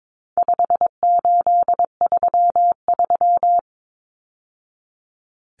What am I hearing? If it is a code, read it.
Morse code, heard as a keyed tone, about 22 wpm, 700 Hz. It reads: 5833
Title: Numbers - 4-Digit 22wpm